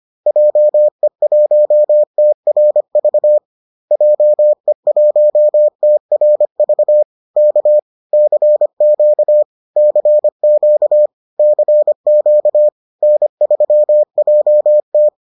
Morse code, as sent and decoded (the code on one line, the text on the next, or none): .--- . .---- - .-. ...- / .--- . .---- - .-. ...- / -.- / -.-. --.- / -.-. --.- / -.-. --.- / -. ...-- .--- -
JE1TRV JE1TRV K CQ CQ CQ N3JT